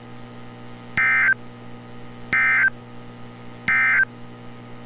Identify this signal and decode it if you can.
Emergency Alert System (SAME) tones, heard as harsh buzzing bursts. It is heard from 0.9 to 4.1 s.
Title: NOAA Weather Radio - Altoona, PA